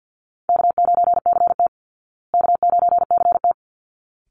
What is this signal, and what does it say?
Morse code, heard as a keyed tone, about 50 wpm, 700 Hz. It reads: K9CT K9CT
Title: Easy Call Signs Repeated 2x - 50wpm